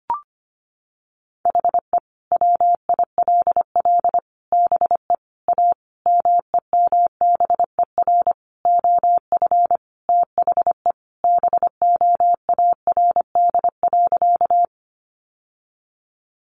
Morse code, heard as a keyed tone, about 25 wpm, 700 Hz. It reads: HE WILL BE A MEMBER OF THE BOARD.
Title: Sentences from Top 1100 Words 25wpm